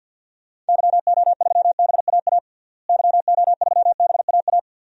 Morse code, heard as a keyed tone, about 47 wpm, 700 Hz. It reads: YO3BAA YO3BAA
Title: DXCC-100 Call Signs - 6 Characters - Repeated 3x - 50wpm